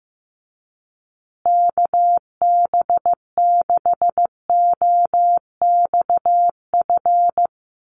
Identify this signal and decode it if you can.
Morse code, heard as a keyed tone, about 15 wpm, 700 Hz. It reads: KB6OXF
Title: USA Call Signs Repeated 2x 15wpm